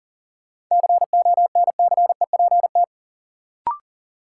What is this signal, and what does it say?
Morse code, heard as a keyed tone, about 40 wpm, 700 Hz. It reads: CONCEPT
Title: Sentences from Top 1100 Words - New Word Drill Alt - 40wpm